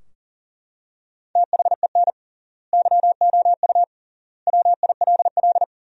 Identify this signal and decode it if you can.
Morse code, heard as a keyed tone, about 40 wpm, 700 Hz. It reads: THEN YOU WILL
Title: 3x Word-Spacing - Sentences from Top 400 Words - 40wpm